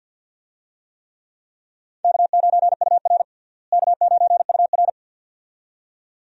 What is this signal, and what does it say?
Morse code, heard as a keyed tone, about 49 wpm, 700 Hz. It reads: K9UR K9UR
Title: USA Call Signs - 4 Characters - Repeated 2x - 50wpm